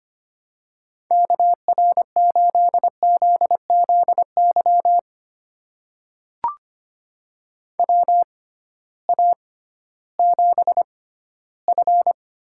Morse code, heard as a keyed tone, about 25 wpm, spaced as for 14 wpm, 700 Hz. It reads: KR8ZZY WA7F
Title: USA Call Signs - 6 Characters - 6x Character Spacing - 25wpm